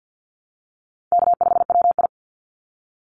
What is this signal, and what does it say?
Morse code, heard as a keyed tone, about 48 wpm, 700 Hz. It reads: K5WI
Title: NA SOTA Chaser Callsigns 50wpm